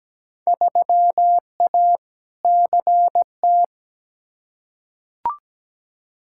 Morse code, heard as a keyed tone, about 17 wpm, 700 Hz. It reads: 3A CT
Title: Field Day Exchanges - 17wpm